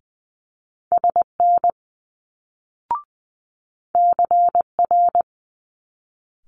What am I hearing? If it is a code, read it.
Morse code, heard as a keyed tone, about 20 wpm, 700 Hz. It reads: SN CR
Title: Multi-letter Phonemes Quick Fire 20wpm